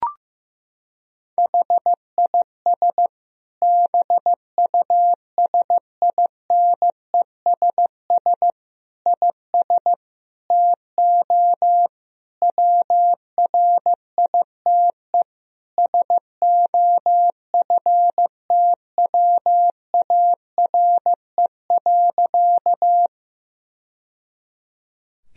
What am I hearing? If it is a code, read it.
Morse code, heard as a keyed tone, about 16 wpm, 700 Hz. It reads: HIS BUSINESS IS TO WRITE SOFTWARE.
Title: Sentences from Top 1000 Words - Review Part 3 - 15wpm